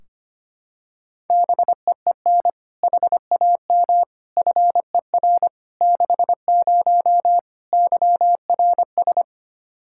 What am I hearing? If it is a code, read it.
Morse code, heard as a keyed tone, about 25 wpm, 700 Hz. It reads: BEEN HAM FER 60 YRS
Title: K1USN SST QSO Practice 25wpm